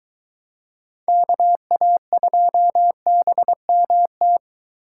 Morse code, heard as a keyed tone, about 23 wpm, 700 Hz. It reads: KA2BMT